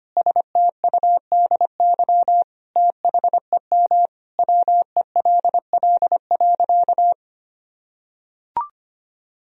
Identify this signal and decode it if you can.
Morse code, heard as a keyed tone, about 25 wpm, 700 Hz. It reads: STUDY THEM WELL.
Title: Sentences from Top 300 Words and Abbreviations 25wpm